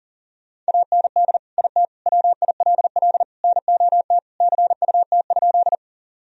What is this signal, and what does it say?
Morse code, heard as a keyed tone, about 37 wpm, 700 Hz. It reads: AND IT WILL NOT CUT?
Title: Sentences from Top 1000 Words - Review Part 1 - 40wpm